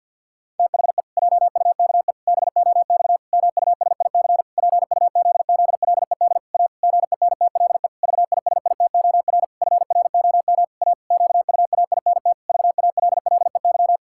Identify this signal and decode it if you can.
Morse code, heard as a keyed tone, about 52 wpm, 700 Hz. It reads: THE JUKE BOX MUSIC PUZZLED A GENTLE VISITOR FROM A QUAINT VALLEY
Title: Pangram Challenge v2 - 50wpm